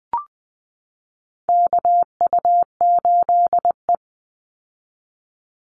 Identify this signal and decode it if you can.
Morse code, heard as a keyed tone, about 20 wpm, 700 Hz. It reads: KU8E
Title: Call Signs Repeated 3x - 20wpm